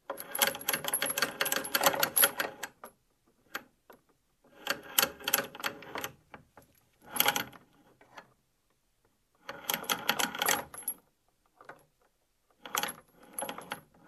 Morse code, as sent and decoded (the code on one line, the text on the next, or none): none